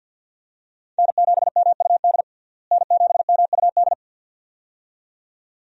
Morse code, heard as a keyed tone, about 52 wpm, 700 Hz. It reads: N7MUD N7MUD
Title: USA Call Signs - 5 Characters - Repeated 2x - 50wpm